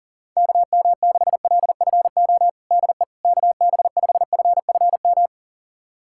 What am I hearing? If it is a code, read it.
Morse code, heard as a keyed tone, about 38 wpm, 700 Hz. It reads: KM6LFO DE KB5FFM
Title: USA Callsign DE Callsign - 6 Characters - 40wpm